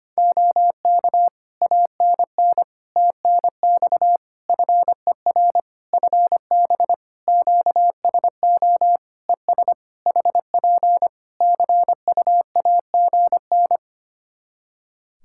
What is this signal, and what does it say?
Morse code, heard as a keyed tone, about 25 wpm, 700 Hz. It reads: OK ANN TNX FER FB QSO ES HP CUAGN
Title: K1USN SST QSO Practice 25wpm